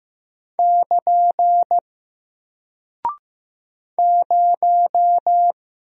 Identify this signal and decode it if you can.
Morse code, heard as a keyed tone, about 15 wpm, 700 Hz. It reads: <KN> 0
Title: Single Character (Letter:Number:Punctuation:Prosign) Alt 15wpm